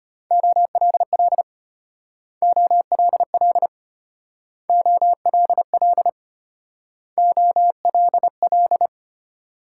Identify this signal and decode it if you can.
Morse code, heard as a keyed tone, about 33 wpm, 700 Hz. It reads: OLL OLL OLL OLL